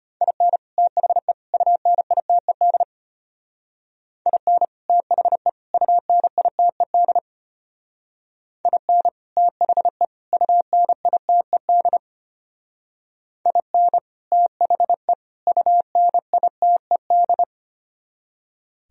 Morse code, heard as a keyed tone, about 32 wpm, 700 Hz. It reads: IN THE UNITED IN THE UNITED IN THE UNITED IN THE UNITED